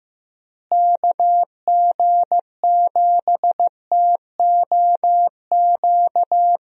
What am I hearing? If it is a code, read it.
Morse code, heard as a keyed tone, about 15 wpm, 700 Hz. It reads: KG7TOQ
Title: USA Call Signs - 6 Characters - Repeated 3x - 15wpm